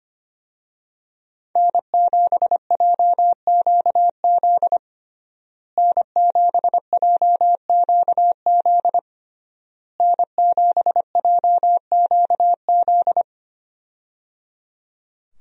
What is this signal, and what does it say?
Morse code, heard as a keyed tone, about 25 wpm, 700 Hz. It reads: N7JQZ N7JQZ N7JQZ
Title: USA Call Signs - 5 Characters - Repeated 3x - 25wpm